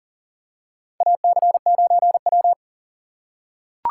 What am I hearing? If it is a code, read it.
Morse code, heard as a keyed tone, about 40 wpm, 700 Hz. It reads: AC9W
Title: USA Call Signs - 4 Characters - 6x Character Spacing - 40wpm